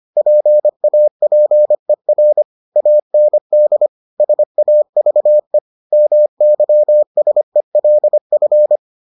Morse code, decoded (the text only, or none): PAPER AND SAVE MYSELF